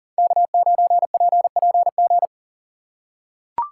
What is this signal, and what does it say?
Morse code, heard as a keyed tone, about 40 wpm, 700 Hz. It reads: K9PPG
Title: USA Call Signs - 5 Characters - 6x Character Spacing - 40wpm